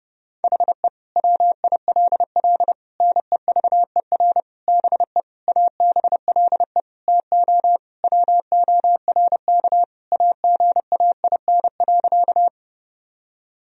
Morse code, heard as a keyed tone, about 30 wpm, 700 Hz. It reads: HE WILL NEVER BE ABLE TO WORK AGAIN.
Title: Sentences from Top 1000 Words - Review Part 2 - 30wpm